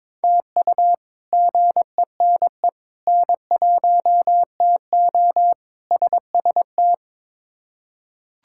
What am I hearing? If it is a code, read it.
Morse code, heard as a keyed tone, about 22 wpm, 700 Hz. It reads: TU GENE N1TO SST
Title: K1USN SST QSO Practice - 22wpm